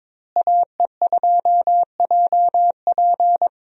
Morse code, heard as a keyed tone, about 22 wpm, 700 Hz. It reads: AE2JP
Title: USA Call Signs - 5 Characters - Repeated 3x - 22wpm